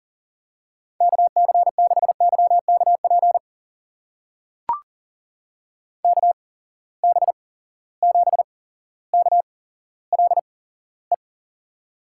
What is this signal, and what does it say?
Morse code, heard as a keyed tone, about 38 wpm, spaced as for 17 wpm, 700 Hz. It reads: KC6YKP KB7KLE
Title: USA Call Signs - 6 Characters - 8x Character Spacing - 40wpm